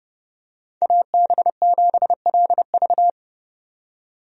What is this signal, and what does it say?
Morse code, heard as a keyed tone, about 32 wpm, 700 Hz. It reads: AB7LV
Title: USA Call Signs - 5 Characters - 30wpm